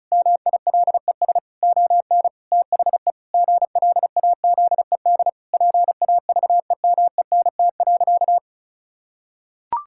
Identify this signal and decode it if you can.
Morse code, heard as a keyed tone, about 36 wpm, 700 Hz. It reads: MILES ON THE GLAZED PAVEMENT.
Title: Pangram Challenge v2 - 35wpm